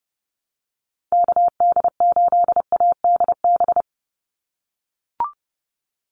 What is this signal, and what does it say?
Morse code, heard as a keyed tone, about 30 wpm, 700 Hz. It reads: KD8ADB
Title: USA Call Signs 30wpm